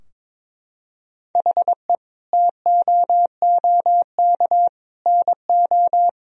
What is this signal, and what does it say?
Morse code, heard as a keyed tone, about 22 wpm, 700 Hz. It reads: HE TOOK NO
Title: Sentences from Top 1000 Words - Review Part 2 - 22wpm